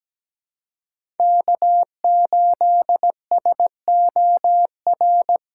Morse code, heard as a keyed tone, about 17 wpm, 700 Hz. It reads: K8SOR
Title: USA Call Signs - 5 Characters - 3x Character Spacing - 17wpm